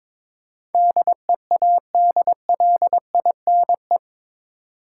Morse code, heard as a keyed tone, about 22 wpm, 700 Hz. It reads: DEADLINE
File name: practice / 22wpm / Eight Letter Words - 22wpm